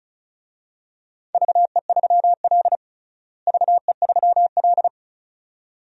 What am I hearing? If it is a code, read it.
Morse code, heard as a keyed tone, about 34 wpm, 700 Hz. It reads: VE3L VE3L
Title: DXCC-100 Call Signs - 4 Characters - Repeated 2x - 35wpm